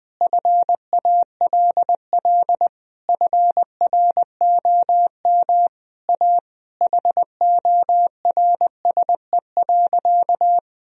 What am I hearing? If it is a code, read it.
Morse code, heard as a keyed tone, about 21 wpm, 700 Hz. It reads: FALL FROM A HORSE.
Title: Sentences from Top 1000 Words - Review Part 2 - 20wpm